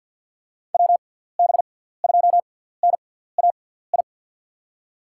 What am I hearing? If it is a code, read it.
Morse code, heard as a keyed tone, about 50 wpm, spaced as for 17 wpm, 700 Hz. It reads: WB2NAI
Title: USA Call Signs - 6 Characters - 6x Character Spacing - 50wpm